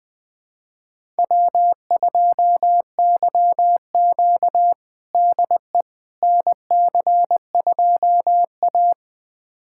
Morse code, heard as a keyed tone, about 20 wpm, 700 Hz. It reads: W2YQ DE NC2A
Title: USA Callsign DE Callsign - 4 Characters - 20wpm